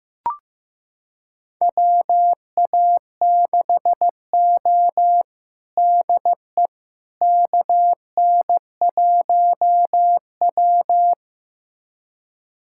Morse code, heard as a keyed tone, about 15 wpm, 700 Hz. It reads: WA6O DE KN1W
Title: USA Callsign DE Callsign - 4 Characters - 15wpm